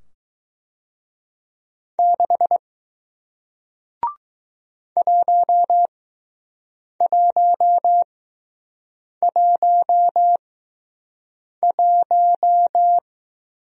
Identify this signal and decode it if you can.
Morse code, heard as a keyed tone, about 20 wpm, spaced as for 6 wpm, 700 Hz. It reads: 6 1111